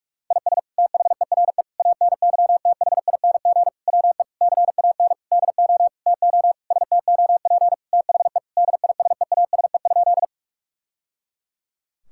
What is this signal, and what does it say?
Morse code, heard as a keyed tone, about 41 wpm, 700 Hz. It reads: IS THERE ANYTHING WE CAN DO TO STOP THE DISEASE?
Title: Sentences from Top 1000 Words - Review Part 6 - 45wpm